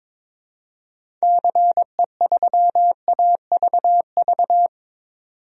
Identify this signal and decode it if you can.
Morse code, heard as a keyed tone, about 22 wpm, 700 Hz. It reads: CE3AVV
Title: DXCC-100 Call Signs - 6 Characters - 22wpm